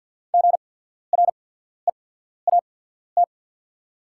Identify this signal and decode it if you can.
Morse code, heard as a keyed tone, about 48 wpm, spaced as for 14 wpm, 700 Hz. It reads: GREAT
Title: 8x Character Spacing - Top 200 Words - 50wpm